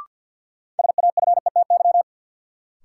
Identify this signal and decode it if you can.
Morse code, heard as a keyed tone, about 48 wpm, 700 Hz. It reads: SAFETY
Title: Sentences from Top 700 Words - New Word Drill - 50wpm